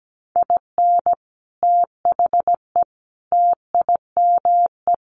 Morse code, heard as a keyed tone, about 17 wpm, 700 Hz. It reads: IN THE TIME
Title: A Sound Of Thunder 17wpm